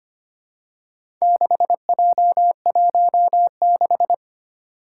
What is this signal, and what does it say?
Morse code, heard as a keyed tone, about 25 wpm, 700 Hz. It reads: 6J16
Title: Four Challenging Letter-Number Combinations 25wpm